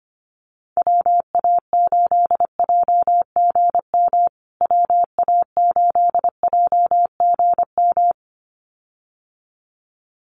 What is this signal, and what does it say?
Morse code, heard as a keyed tone, about 25 wpm, 700 Hz. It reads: WA8JGM WA8JGM
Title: USA Call Signs Repeated 2x 25wpm